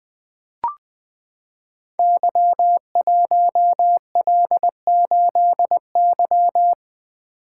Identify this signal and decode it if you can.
Morse code, heard as a keyed tone, about 20 wpm, 700 Hz. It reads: Y1L8Y
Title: Code Groups - 5 Letter:Numbers - 20wpm